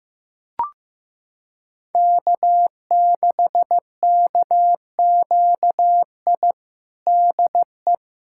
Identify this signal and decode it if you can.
Morse code, heard as a keyed tone, about 15 wpm, 700 Hz. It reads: K6KQI DE
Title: USA Callsign DE Callsign - 5 Characters - 15wpm